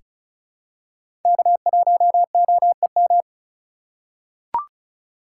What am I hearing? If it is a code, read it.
Morse code, heard as a keyed tone, about 36 wpm, 700 Hz. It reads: K1OEM
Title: USA Call Signs - 5 Characters - 2x Character Spacing - 35wpm